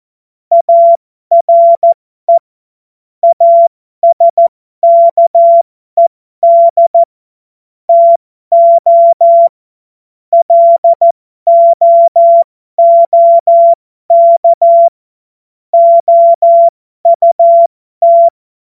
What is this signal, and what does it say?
Morse code, heard as a keyed tone, about 14 wpm, spaced as for 12 wpm, 680 Hz. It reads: ARE ASKED TO LOOK OUT
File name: Morsekurs / Witze